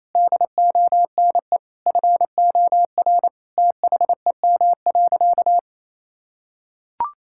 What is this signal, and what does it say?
Morse code, heard as a keyed tone, about 28 wpm, 700 Hz. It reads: DONE FOR THEM.
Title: Sentences from Top 1600 Words 28wpm